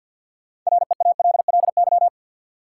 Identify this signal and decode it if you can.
Morse code, heard as a keyed tone, about 49 wpm, 700 Hz. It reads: REALLY
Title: Sentences from Top 600 Words - New Word Drill - 50wpm